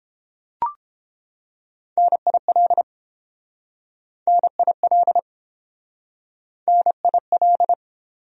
Morse code, heard as a keyed tone, about 29 wpm, 700 Hz. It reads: NIL NIL NIL